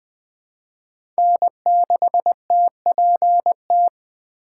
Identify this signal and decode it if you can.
Morse code, heard as a keyed tone, about 21 wpm, 700 Hz. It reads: N6TPT
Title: USA Call Signs - 8x Character Spacing - 20wpm